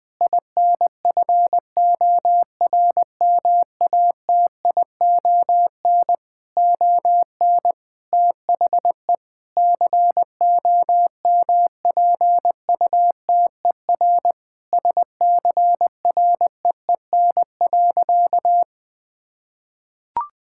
Morse code, heard as a keyed tone, about 21 wpm, 700 Hz. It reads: INFORMATION ON THE COMPUTER SCREEN.
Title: Sentences from Top 1000 Words - Review Part 6 - 20wpm